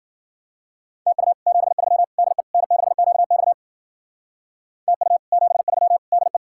Morse code, heard as a keyed tone, about 48 wpm, 700 Hz. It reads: TU 73 DE N6QX TU 73 DE
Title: Speed-Racing Common QSO Phrases - 40wpm